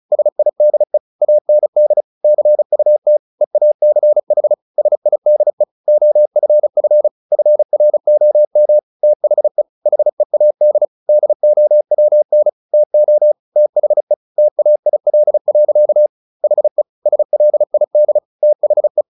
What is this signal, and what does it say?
Morse code, heard as a keyed tone, about 34 wpm, 600 Hz. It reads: SIDE AND CUT EACH SIDE OFF FROM THE HEAD DOWN TO THE TAIL. HE SLID THE
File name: the old man and the sea 2 35wpm